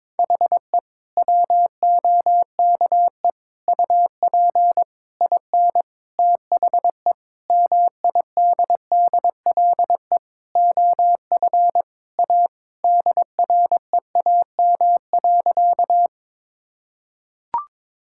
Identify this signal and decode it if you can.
Morse code, heard as a keyed tone, about 22 wpm, 700 Hz. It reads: HE WOKE UP IN THE MIDDLE OF A DREAM.
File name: practice / 22wpm / Sentences with 4-Word Phrases - 22wpm